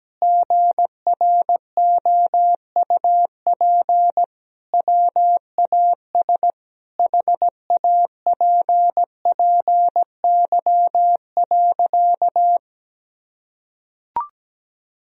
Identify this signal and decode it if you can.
Morse code, heard as a keyed tone, about 17 wpm, 700 Hz. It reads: GROUP WAS HAPPY.
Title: Sentences from Top 1000 Words - Review Part 1 - 17wpm